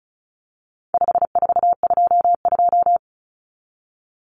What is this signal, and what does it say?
Morse code, heard as a keyed tone, about 34 wpm, 700 Hz. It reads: 5422
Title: Numbers - 4-Digit 35wpm